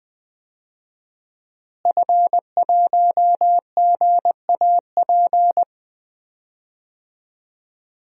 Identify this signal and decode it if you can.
Morse code, heard as a keyed tone, about 20 wpm, 700 Hz. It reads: F1GAP
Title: DXCC-100 Call Signs - 5 Characters - Repeated 3x - 20wpm